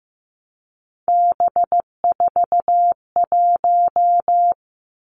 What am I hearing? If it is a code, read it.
Morse code, heard as a keyed tone, about 15 wpm, 700 Hz. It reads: B41
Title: Three Challenging Letter-Number Combinations 15wpm